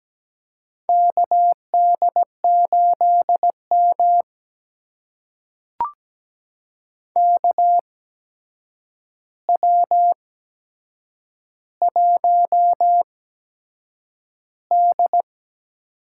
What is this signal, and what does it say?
Morse code, heard as a keyed tone, about 17 wpm, spaced as for 7 wpm, 700 Hz. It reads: KD8M KW1D
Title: USA Call Signs - 4 Characters - 8x Character Spacing - 17wpm